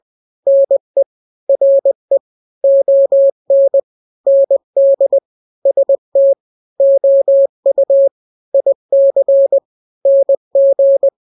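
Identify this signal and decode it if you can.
Morse code, heard as a keyed tone, about 20 wpm, 550 Hz. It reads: NE RE ON ND ST OU IC NG
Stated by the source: Morse drill of two-letter n-grams at 20/18 WPM (Farnsworth), from KK7RR.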